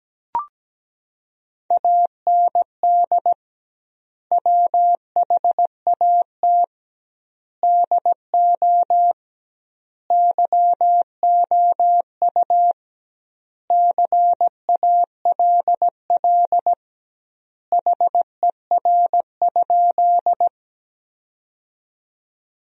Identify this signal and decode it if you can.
Morse code, heard as a keyed tone, about 17 wpm, 700 Hz. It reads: AND WHAT DO YOU CALL HER?
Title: 2x Word-Spacing - Sentences from Top 100 Words - 17wpm